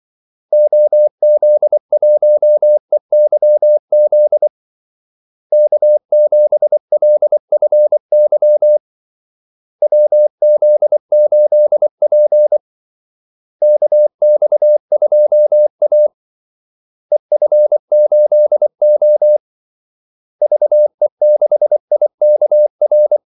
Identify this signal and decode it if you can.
Morse code, heard as a keyed tone, about 24 wpm, 600 Hz. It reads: OZ1EYZ K7LFY WZ8P KX2A EF8O VE6IKR
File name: Callsigns-24wpm